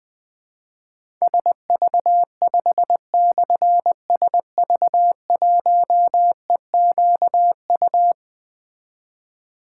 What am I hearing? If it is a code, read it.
Morse code, heard as a keyed tone, about 20 wpm, 700 Hz. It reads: SV5/SV1EQU
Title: Difficult Call Signs 20wpm